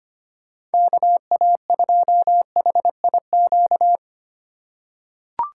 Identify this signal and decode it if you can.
Morse code, heard as a keyed tone, about 25 wpm, 700 Hz. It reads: KA2HIQ
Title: USA Call Signs - 6 Characters - 3x Character Spacing - 25wpm